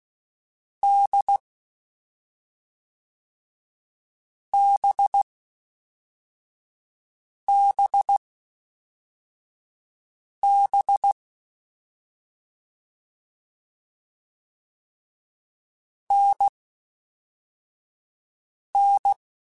Morse code, heard as a keyed tone, about 16 wpm, 800 Hz. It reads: D B B B N N
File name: Difficult Combos / Difficult Combos-mp3s / NDB NDB6 @5